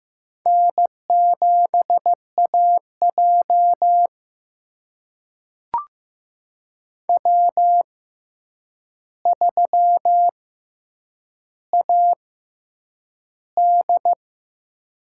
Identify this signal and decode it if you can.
Morse code, heard as a keyed tone, about 15 wpm, spaced as for 8 wpm, 700 Hz. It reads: N7AJ W3AD
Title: USA Call Signs - 6x Character Spacing - 15wpm